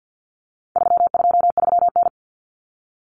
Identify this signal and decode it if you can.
Morse code, heard as a keyed tone, about 50 wpm, 700 Hz. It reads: 323N
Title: Common Cut Numbers - 4-Digit 50wpm